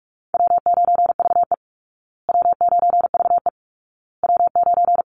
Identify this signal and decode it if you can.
Morse code, heard as a keyed tone, about 44 wpm, 700 Hz. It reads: W9VE W9VE W9
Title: Call Signs Repeated 2x - 45wpm